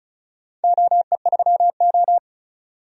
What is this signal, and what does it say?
Morse code, heard as a keyed tone, about 36 wpm, 700 Hz. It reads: OE3O
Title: DXCC-100 Call Signs - 4 Characters - 35wpm